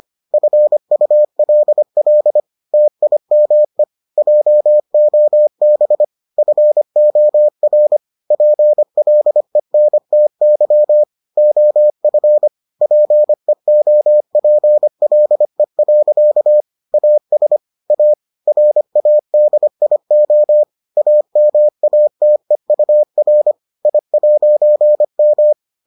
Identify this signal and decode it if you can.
Morse code, heard as a keyed tone, about 25 wpm, 600 Hz. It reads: FULL TIME JOB FOR PLENTY OF PEOPLE. AS A RADIO AMATEUR I'M